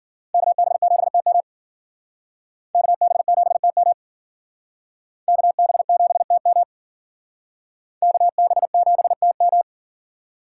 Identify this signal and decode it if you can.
Morse code, heard as a keyed tone, about 51 wpm, 700 Hz. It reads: KB7TM KB7TM KB7TM KB7TM